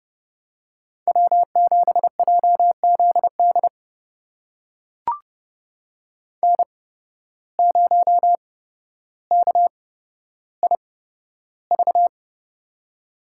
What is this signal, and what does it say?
Morse code, heard as a keyed tone, about 31 wpm, spaced as for 13 wpm, 700 Hz. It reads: W7JZD N0KIV
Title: USA Call Signs - 5 Characters - 8x Character Spacing - 30wpm